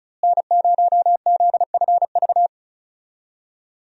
Morse code, heard as a keyed tone, about 34 wpm, 700 Hz. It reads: N0ZFV
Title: USA Call Signs - 5 Characters - Repeated 2x - 35wpm